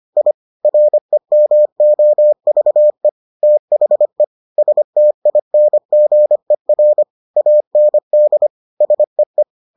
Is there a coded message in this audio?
Yes: Morse code from 0.2 to 9.4 s, heard as a keyed tone, about 25 wpm, 600 Hz. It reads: I REMOVE THE STINGER AND SEE